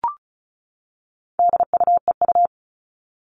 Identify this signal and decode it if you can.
Morse code, heard as a keyed tone, about 34 wpm, 700 Hz. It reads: DUEU
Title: All Cut Numbers - 4-Digit 35wpm